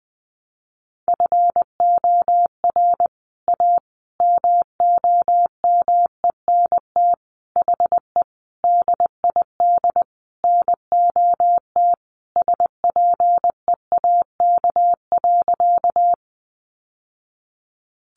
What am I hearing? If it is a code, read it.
Morse code, heard as a keyed tone, about 20 wpm, 700 Hz. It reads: FOR A MOMENT HE DID NOT SPEAK.
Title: Sentences from Top 1100 Words 20wpm